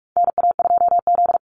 Morse code, heard as a keyed tone, about 44 wpm, 700 Hz. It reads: NA2Z
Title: USA Call Signs 45wpm